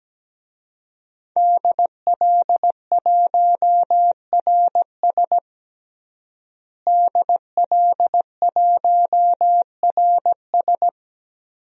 Morse code, heard as a keyed tone, about 17 wpm, 700 Hz. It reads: DL1RS DL1RS
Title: DXCC-100 Call Signs - 5 Characters - Repeated 3x - 17wpm